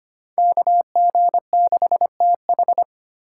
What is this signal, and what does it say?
Morse code, heard as a keyed tone, about 33 wpm, 700 Hz. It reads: KG6TH